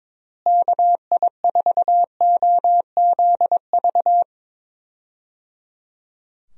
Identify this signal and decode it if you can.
Morse code, heard as a keyed tone, about 21 wpm, 700 Hz. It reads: KI4OZV
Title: USA Call Signs - 6 Characters - Repeated 3x - 22wpm